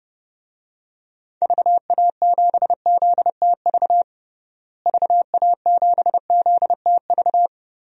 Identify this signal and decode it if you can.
Morse code, heard as a keyed tone, about 32 wpm, 700 Hz. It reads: VA7ZTV VA7ZTV
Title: DXCC-100 Call Signs - 6 Characters - Repeated 2x - 30wpm